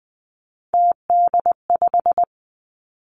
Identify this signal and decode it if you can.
Morse code, heard as a keyed tone, about 19 wpm, 700 Hz. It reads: TD5
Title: Three Challenging Letter-Number Combinations 20wpm